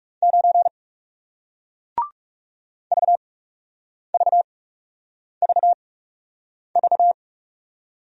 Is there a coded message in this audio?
Yes: Morse code from 0.2 to 7.1 s, heard as a keyed tone, about 39 wpm, spaced as for 8 wpm, 700 Hz. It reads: 9 VVVV